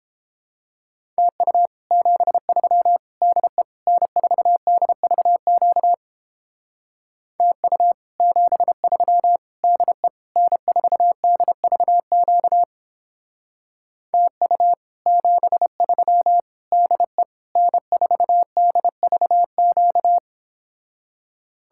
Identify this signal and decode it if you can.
Morse code, heard as a keyed tone, about 29 wpm, 700 Hz. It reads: TU 73 DE N4DVQ TU 73 DE N4DVQ TU 73 DE N4DVQ